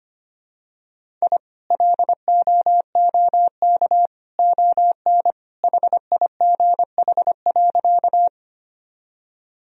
Morse code, heard as a keyed tone, about 25 wpm, 700 Hz. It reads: I LOOK ON HIGH.
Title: Sentences from Top 1000 Words - Review Part 1 - 25wpm